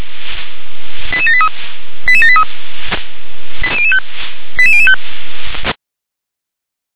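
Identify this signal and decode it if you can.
Five-tone selective call (ZVEI), heard as quick runs of five tones, tones of about 70 ms. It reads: 88783 88783 88005 88005
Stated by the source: ZVEI five-tone call 88078 received by SDR.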